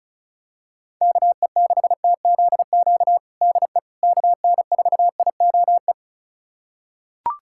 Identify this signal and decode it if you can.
Morse code, heard as a keyed tone, about 36 wpm, 700 Hz. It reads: KE6TZQ DE KN4IOE
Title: USA Callsign DE Callsign 35wpm